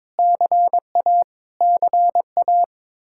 Morse code, heard as a keyed tone, about 22 wpm, 700 Hz. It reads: CA CA
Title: Practice QSO - Exchange 1 - Beginning - 22wpm